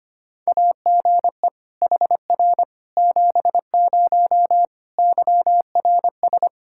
Morse code, heard as a keyed tone, about 25 wpm, 700 Hz. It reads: AGE HR 70 YRS
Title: Practice QSO - Exchange 3 - More Info - 25wpm